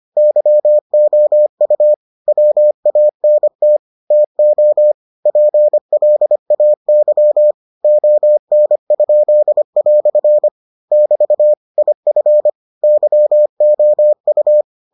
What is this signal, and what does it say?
Morse code, heard as a keyed tone, about 25 wpm, 600 Hz. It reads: YOU WANT TO PLAY ON?" <BT> IF YOU